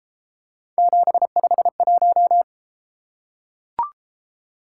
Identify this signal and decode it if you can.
Morse code, heard as a keyed tone, about 33 wpm, 700 Hz. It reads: 751